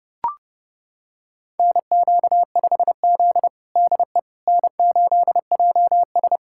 Morse code, heard as a keyed tone, about 32 wpm, 700 Hz. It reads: NQ5Z DE N8JS
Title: USA Callsign DE Callsign - 4 Characters - 30wpm